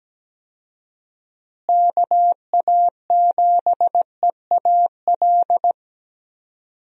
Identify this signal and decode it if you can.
Morse code, heard as a keyed tone, about 17 wpm, 700 Hz. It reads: KA7EAL
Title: USA Call Signs - 6 Characters - Repeated 3x - 17wpm